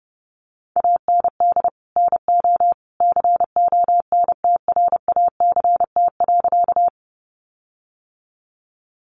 Morse code, heard as a keyed tone, about 30 wpm, 700 Hz. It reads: AND NO CONTRACT.
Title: Sentences from Top 1400 Words 30wpm